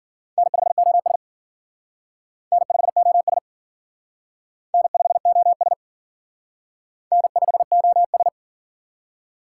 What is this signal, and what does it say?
Morse code, heard as a keyed tone, about 50 wpm, 700 Hz. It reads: N5OS N5OS N5OS N5OS